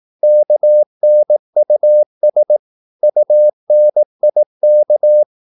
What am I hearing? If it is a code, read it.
Morse code, heard as a keyed tone, about 18 wpm, 600 Hz. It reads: KNUS UNIK
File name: Korte ord DK 18wpm